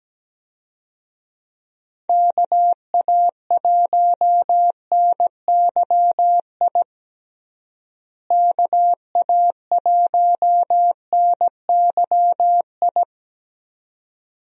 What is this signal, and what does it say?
Morse code, heard as a keyed tone, about 17 wpm, 700 Hz. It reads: KA1NYI KA1NYI
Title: USA Call Signs - 6 Characters - Repeated 2x - 17wpm